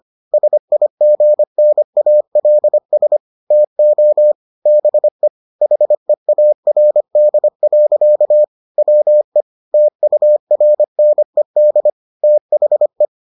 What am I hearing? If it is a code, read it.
Morse code, heard as a keyed tone, about 25 wpm, 600 Hz. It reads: SIGNALS TO BE HEARD. WE TURNED THE